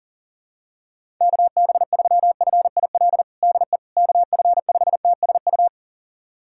Morse code, heard as a keyed tone, about 38 wpm, 700 Hz. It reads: KB3FIL DE KF5TSU
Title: USA Callsign DE Callsign - 6 Characters - 40wpm